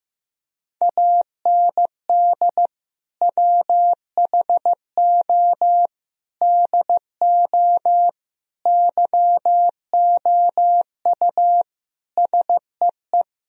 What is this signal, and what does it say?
Morse code, heard as a keyed tone, about 16 wpm, 700 Hz. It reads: AND WHO DO YOU SEE
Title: Sentences from Top 1000 Words - Review Part 3 - 15wpm